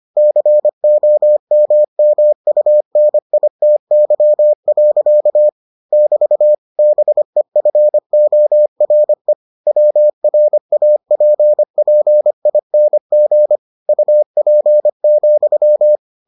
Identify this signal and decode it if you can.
Morse code, heard as a keyed tone, about 25 wpm, 600 Hz. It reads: COMMUNITY. <BT> BEFORE WRAPPING UP,